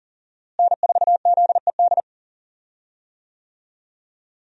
Morse code, heard as a keyed tone, about 37 wpm, 700 Hz. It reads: N4ZED
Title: USA Call Signs - 5 Characters - Repeated 3x - 40wpm